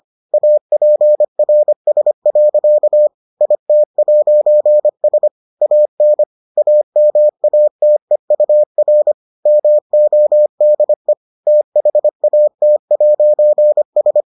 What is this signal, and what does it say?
Morse code, heard as a keyed tone, about 25 wpm, 600 Hz. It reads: APRS. IT'S AN AMATEUR MODE THAT'S